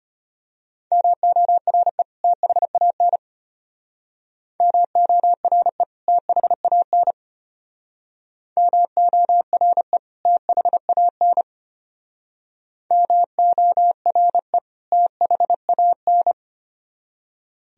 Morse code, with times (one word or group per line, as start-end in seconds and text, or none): 0.9-2.0 MORE
2.2-3.2 THAN
4.6-5.8 MORE
6.1-7.1 THAN
8.6-10.0 MORE
10.3-11.4 THAN
12.9-14.6 MORE
14.9-16.3 THAN